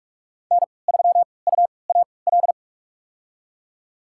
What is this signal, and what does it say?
Morse code, heard as a keyed tone, about 46 wpm, spaced as for 26 wpm, 700 Hz. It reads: N3UAL
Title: USA Call Signs - 5 Characters - 3x Character Spacing - 45wpm